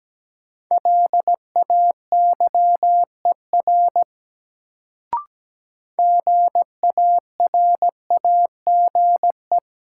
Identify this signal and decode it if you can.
Morse code, heard as a keyed tone, about 17 wpm, 700 Hz. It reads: LAYER GARAGE
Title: Sentences from Top 1600 Words - New Word Drill - 17wpm